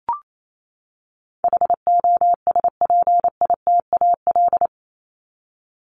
Morse code, heard as a keyed tone, about 28 wpm, 700 Hz. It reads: HOSPITAL
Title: Top 1751-2000 Words 28wpm